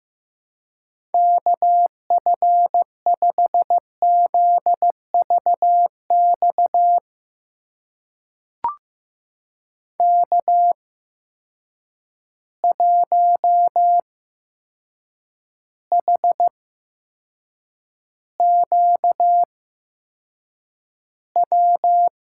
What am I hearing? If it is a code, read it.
Morse code, heard as a keyed tone, about 15 wpm, spaced as for 7 wpm, 700 Hz. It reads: KF5ZVX K1HQW
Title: USA Call Signs - 8x Character Spacing - 15wpm